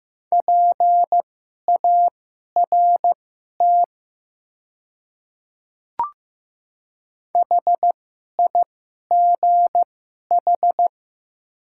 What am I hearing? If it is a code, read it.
Morse code, heard as a keyed tone, about 15 wpm, spaced as for 11 wpm, 700 Hz. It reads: PART HIGH
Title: 2x Character Spacing - Top 500 Words - 15wpm